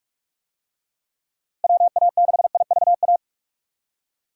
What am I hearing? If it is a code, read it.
Morse code, heard as a keyed tone, about 47 wpm, 700 Hz. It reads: WA6IVA
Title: USA Call Signs - 6 Characters - Repeated 3x - 45wpm